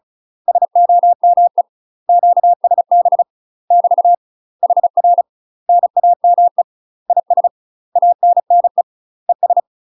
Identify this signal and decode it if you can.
Morse code, heard as a keyed tone, about 36 wpm, 700 Hz. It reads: SOME QSB <BT> HR NAME IS ANNE ES
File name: UK MASTER QSOs TRAINING / 35WPM / UK MASTER QSOs 0038